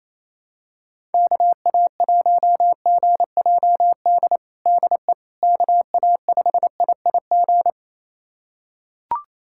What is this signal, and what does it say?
Morse code, heard as a keyed tone, about 28 wpm, 700 Hz. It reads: KA1GJD DE KA5IIG
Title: USA Callsign DE Callsign - 6 Characters - 28wpm